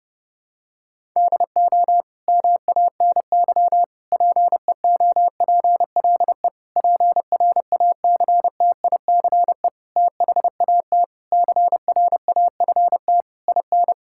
Morse code, heard as a keyed tone, about 32 wpm, 700 Hz. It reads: DO MANY PEOPLE PRACTICE THAT CRAFT IN
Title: Sentences from Top 1000 Words - Review Part 6 - 30wpm